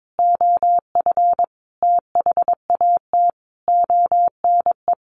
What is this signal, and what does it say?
Morse code, heard as a keyed tone, about 22 wpm, 700 Hz. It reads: OF THAT ONE
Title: A Sound Of Thunder 22wpm